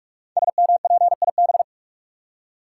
Morse code, heard as a keyed tone, about 44 wpm, 700 Hz. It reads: SMPIB
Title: Code Groups - 5 Letters - 45wpm